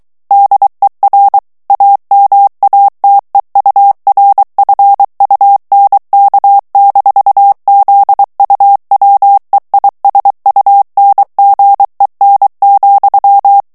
Morse code, heard as a keyed tone, about 23 wpm, 800 Hz. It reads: DER AMATEURFUNK-ZUWEISUNGEN,
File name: ALLEMAND texte clair 1 vitesse 22 mots minute